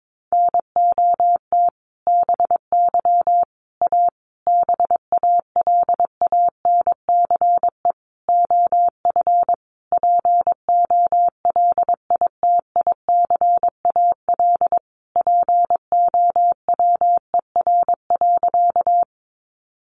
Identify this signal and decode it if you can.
Morse code, heard as a keyed tone, about 22 wpm, 700 Hz. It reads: NOT BY A BALANCE OF POLITICAL POWER.
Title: Sentences from Top 900 Words 22wpm